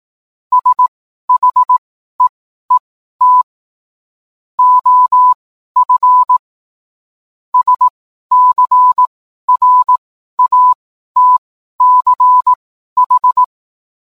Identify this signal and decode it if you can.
Morse code, heard as a keyed tone, about 18 wpm, spaced as for 12 wpm, 1000 Hz. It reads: SHEET OF SCRATCH